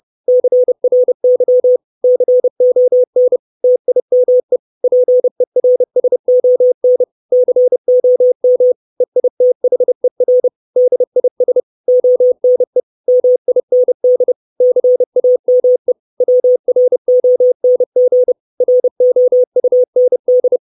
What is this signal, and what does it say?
Morse code, heard as a keyed tone, about 30 wpm, 500 Hz. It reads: CRY CON TIME PERSON COM EITHER DIS ONE MIND CAME WRONG ROUND
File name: 632 Words 30WPM